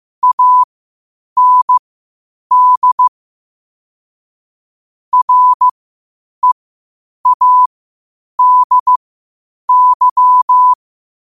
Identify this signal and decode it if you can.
Morse code, heard as a keyed tone, about 15 wpm, spaced as for 8 wpm, 1000 Hz. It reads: AND READY
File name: qotdfiles / 8wpm